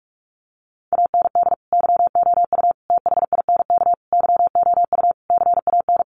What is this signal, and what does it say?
Morse code, heard as a keyed tone, about 44 wpm, 700 Hz. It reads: AND YOU THINK YOU CAN
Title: Sentences from Top 1300 Words 45wpm